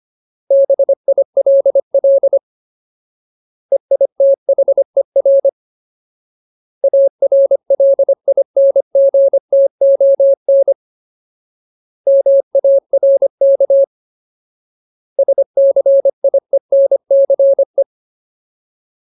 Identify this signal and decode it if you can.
Morse code, heard as a keyed tone, about 25 wpm, 550 Hz. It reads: BILL EITHER ARLINGTON MARK SCIENCE
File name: Ultimate Word List / 25WPM Ultimate